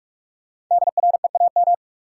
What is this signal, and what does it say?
Morse code, heard as a keyed tone, about 46 wpm, 700 Hz. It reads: DREAM
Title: Sentences from Top 1500 Words - New Word Drill Alt - 45wpm